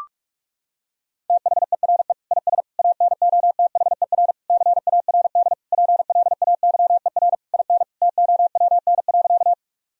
Morse code, heard as a keyed tone, about 41 wpm, 700 Hz. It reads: THERE IS ANOTHER CARD PLAYER IN TOWN.
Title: Sentences from Top 1000 Words - Review Part 4 - 45wpm